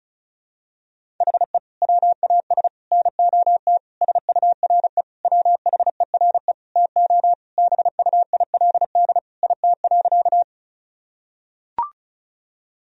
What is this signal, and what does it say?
Morse code, heard as a keyed tone, about 36 wpm, 700 Hz. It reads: HE WAS NOT SURE WHERE TO BUILD IT.
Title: Sentences from Top 1000 Words - Review Part 1 - 35wpm